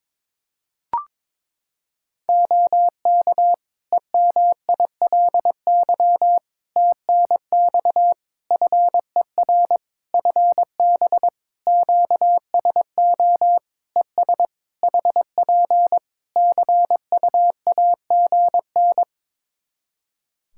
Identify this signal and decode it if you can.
Morse code, heard as a keyed tone, about 22 wpm, 700 Hz. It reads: OK EMILY TNX FER FB QSO ES HP CUAGN
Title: Practice QSO - Exchange 4 - Ending - 22wpm